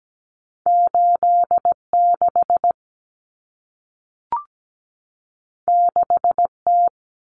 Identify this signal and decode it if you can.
Morse code, heard as a keyed tone, about 17 wpm, 700 Hz. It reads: 86 6T
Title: Common Cut Numbers - 2-Digit 17wpm